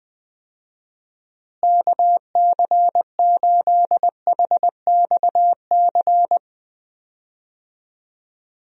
Morse code, heard as a keyed tone, about 21 wpm, 700 Hz. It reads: KC8HXC
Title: USA Call Signs - 6 Characters - Repeated 2x - 20wpm